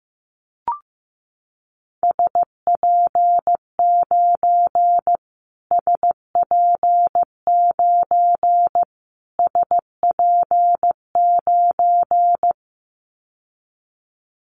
Morse code, heard as a keyed tone, about 15 wpm, 700 Hz. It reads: SP9 SP9 SP9
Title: Callsign Prefixes for Top 25 DXCC - Repeated 3x - 15wpm